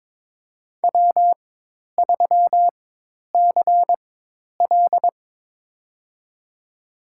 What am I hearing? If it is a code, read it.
Morse code, heard as a keyed tone, about 22 wpm, spaced as for 10 wpm, 700 Hz. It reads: W3CL
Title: USA Call Signs - 4 Characters - 4x Character Spacing - 22wpm